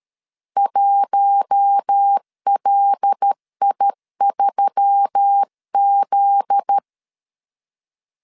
Morse code, heard as a keyed tone, about 13 wpm, 800 Hz. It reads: 1LI3Z
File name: LEZIONI / 7 GF9Z / 7 GF9Z 12wpm 2147